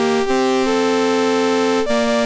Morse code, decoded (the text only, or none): none